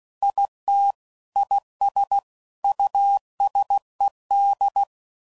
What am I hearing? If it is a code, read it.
Morse code, heard as a keyed tone, about 16 wpm, 800 Hz. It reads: IT IS USED
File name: CW Code Groups @15WPM / Technical Subjects / Reactance